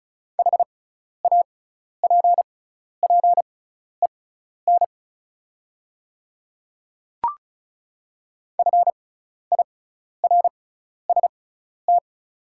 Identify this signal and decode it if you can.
Morse code, heard as a keyed tone, about 36 wpm, spaced as for 12 wpm, 700 Hz. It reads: HAPPEN FIRST
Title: 6x Character Spacing - Top 500 Words - 35wpm